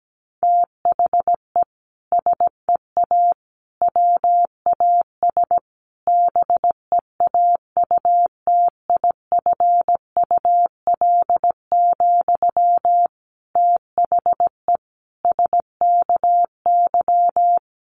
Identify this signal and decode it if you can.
Morse code, heard as a keyed tone, about 17 wpm, 700 Hz. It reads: THE SEA WAS BEAUTIFUL, THE SKY
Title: Sentences from Top 1200 Words 17wpm